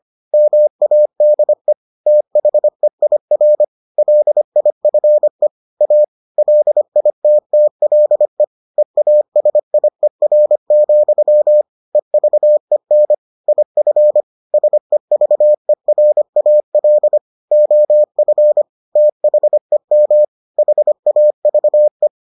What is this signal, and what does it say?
Morse code, heard as a keyed tone, about 25 wpm, 600 Hz. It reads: MADE THEIR LIFE A LITTLE EASIER, EVEN IF SEVERAL OF THEM HAVE